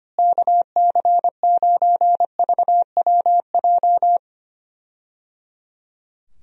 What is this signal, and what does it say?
Morse code, heard as a keyed tone, about 25 wpm, 700 Hz. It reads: KC9VWJ
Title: USA Call Signs - 6 Characters - Repeated 3x - 25wpm